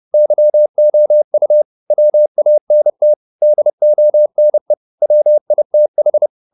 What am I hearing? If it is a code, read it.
Morse code, heard as a keyed tone, about 30 wpm, 600 Hz. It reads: YOU WANT DONE WITH